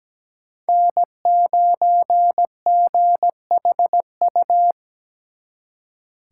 Morse code, heard as a keyed tone, about 20 wpm, 700 Hz. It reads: N9GHU
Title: Speed Racing - USA Call Signs 17wpm